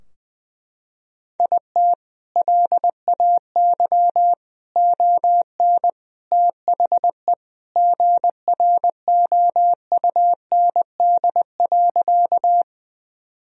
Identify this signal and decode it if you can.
Morse code, heard as a keyed tone, about 20 wpm, 700 Hz. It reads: IT LAY ON THE GROUND.
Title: Sentences from Top 1000 Words - Review Part 2 - 20wpm